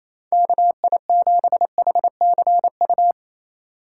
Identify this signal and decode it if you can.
Morse code, heard as a keyed tone, about 28 wpm, 700 Hz. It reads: KI7HCU
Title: USA Call Signs - 6 Characters - 2x Character Spacing - 28wpm